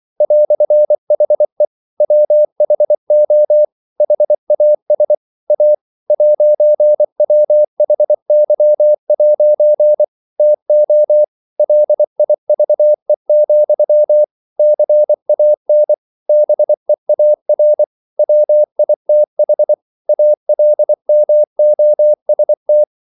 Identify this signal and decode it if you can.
Morse code, heard as a keyed tone, about 24 wpm, 600 Hz. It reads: "HE WHO HAS A 'WHY' TO LIVE, CAN BEAR WITH ALMOST
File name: CW / Practice / EnglishQuotations0224